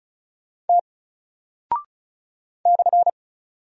Morse code, heard as a keyed tone, about 36 wpm, 700 Hz. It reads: T /
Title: Single Character (Letter:Number:Punctuation:Prosign) Alt 35wpm